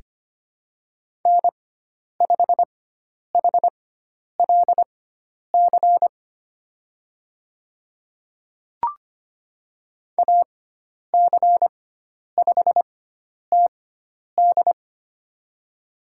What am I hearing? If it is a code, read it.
Morse code, heard as a keyed tone, about 25 wpm, spaced as for 10 wpm, 700 Hz. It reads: N5HLC AC5TD
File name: Lesson 16g - Letter c - Call Signs 25wpm